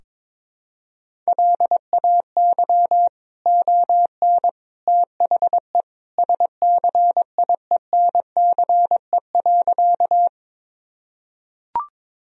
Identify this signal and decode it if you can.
Morse code, heard as a keyed tone, about 22 wpm, 700 Hz. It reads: LAY ON THE SCIENCE.
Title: Sentences from Top 1000 Words - Review Part 2 - 22wpm